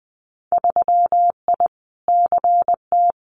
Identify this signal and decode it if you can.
Morse code, heard as a keyed tone, about 20 wpm, 700 Hz. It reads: 3I CT
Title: Winter Field Day Exchanges 20wpm